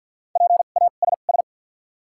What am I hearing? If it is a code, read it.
Morse code, heard as a keyed tone, about 48 wpm, spaced as for 35 wpm, 700 Hz. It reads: PASS
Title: 2x Character Spacing - Top 500 Words - 50wpm